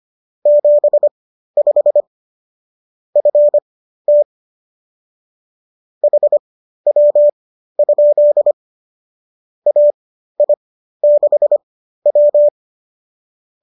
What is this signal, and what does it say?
Morse code, heard as a keyed tone, about 25 wpm, spaced as for 13 wpm, 590 Hz. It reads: 75 FT HW? AI6W
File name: qso207 13